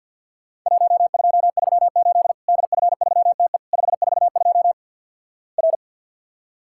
Morse code, heard as a keyed tone, about 46 wpm, 700 Hz. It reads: 1238 DF3TE 542 R
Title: Worked All Europe DX Contest - QTC Exchange - 50wpm